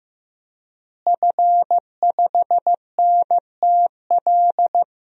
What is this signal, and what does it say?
Morse code, heard as a keyed tone, about 16 wpm, 700 Hz. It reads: F5NTL
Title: DXCC-100 Call Signs - 5 Characters - Repeated 2x - 15wpm